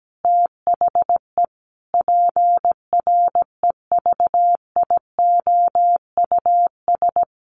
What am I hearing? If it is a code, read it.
Morse code, heard as a keyed tone, about 17 wpm, 700 Hz. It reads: THE PREVIOUS
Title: Sentences from Top 1500 Words 17wpm